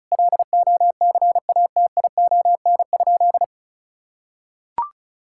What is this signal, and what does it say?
Morse code, heard as a keyed tone, about 36 wpm, 700 Hz. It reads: LOCATION?
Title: Sentences from Top 1000 Words - Review Part 6 - 35wpm